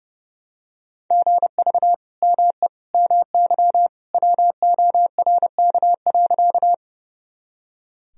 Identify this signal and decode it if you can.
Morse code, heard as a keyed tone, about 30 wpm, 700 Hz. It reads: GV ME MY WORK.
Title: Sentences from Top 200 Words and Abbreviations 30wpm